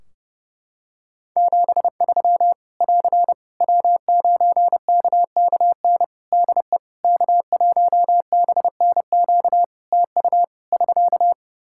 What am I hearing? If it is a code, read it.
Morse code, heard as a keyed tone, about 30 wpm, 700 Hz. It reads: 73 <AR> W9KKN DE K1BNQ TU <SK>
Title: K1USN SST QSO Practice 30wpm